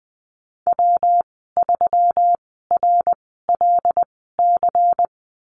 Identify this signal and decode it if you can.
Morse code, heard as a keyed tone, about 20 wpm, spaced as for 15 wpm, 700 Hz. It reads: W3RLC
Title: 147 Lesson 24g - Letter y - Call Signs 20wpm